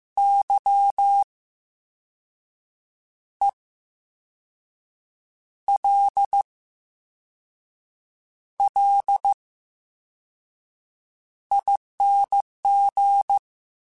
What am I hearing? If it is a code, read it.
Morse code, heard as a keyed tone, about 15 wpm, spaced as for 5 wpm, 800 Hz. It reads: YELLING
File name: Special INGs-mp3s / Special INGs-mp3s / VerySlowToFaster Ing @13